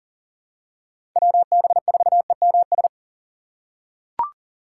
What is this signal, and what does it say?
Morse code, heard as a keyed tone, about 40 wpm, 700 Hz. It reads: WB4EMS
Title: USA Call Signs - 6 Characters - 3x Character Spacing - 40wpm